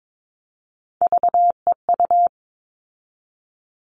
Morse code, heard as a keyed tone, about 22 wpm, 700 Hz. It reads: VEU
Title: All Cut Numbers - 3-Digit 22wpm